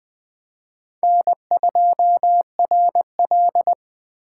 Morse code, heard as a keyed tone, about 20 wpm, 700 Hz. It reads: N2RL